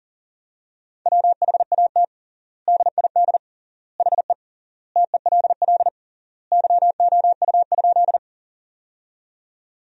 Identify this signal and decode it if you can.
Morse code, heard as a keyed tone, about 40 wpm, 700 Hz. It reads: WHAT DID HE TELL YOU?
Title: 3x Word-Spacing - Sentences from Top 200 Words - 40wpm